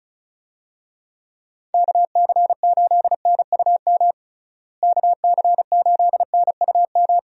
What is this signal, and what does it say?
Morse code, heard as a keyed tone, about 34 wpm, 700 Hz. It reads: KC8NUM KC8NUM
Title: USA Call Signs - 6 Characters - Repeated 3x - 35wpm